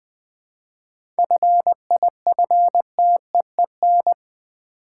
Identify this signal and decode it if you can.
Morse code, heard as a keyed tone, about 20 wpm, 700 Hz. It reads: FIFTEEN
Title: 7 Letter Words - First 3 Letters Given - 20wpm